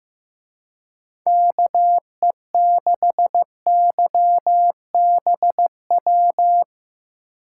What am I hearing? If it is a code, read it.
Morse code, heard as a keyed tone, about 17 wpm, 700 Hz. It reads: KE6YBW
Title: Speed Racing - USA 2x3 Call Signs 15wpm